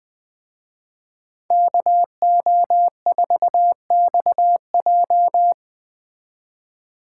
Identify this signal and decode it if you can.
Morse code, heard as a keyed tone, about 20 wpm, 700 Hz. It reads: KO4XJ